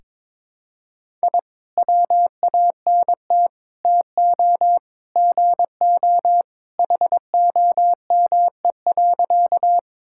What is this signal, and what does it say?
Morse code, heard as a keyed tone, about 23 wpm, 700 Hz. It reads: I WANT TO GO HOME.
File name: practice / 22wpm / Sentences with 3-Word Phrases - 22wpm